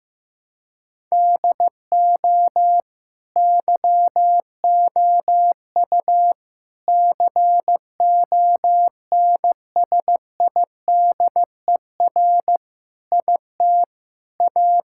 Morse code, heard as a keyed tone, about 15 wpm, 700 Hz. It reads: DO YOU CONSIDER IT A
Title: Sentences from Top 1000 Words - Review Part 6 - 15wpm